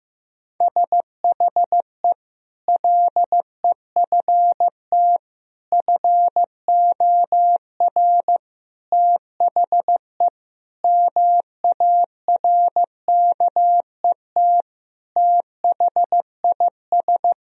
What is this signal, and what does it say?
Morse code, heard as a keyed tone, about 16 wpm, 700 Hz. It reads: SHE LEFT FOR THE MARKET THIS
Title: Sentences from Top 1000 Words - Review Part 3 - 15wpm